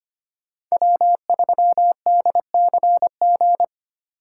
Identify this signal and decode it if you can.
Morse code, heard as a keyed tone, about 33 wpm, 700 Hz. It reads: W3DCG